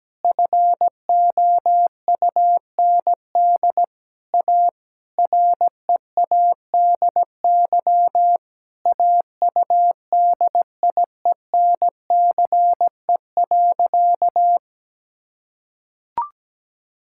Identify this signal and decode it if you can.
Morse code, heard as a keyed tone, about 17 wpm, 700 Hz. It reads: FOUND A READY AUDIENCE.
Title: Sentences from Top 1000 Words - Review Part 5 - 17wpm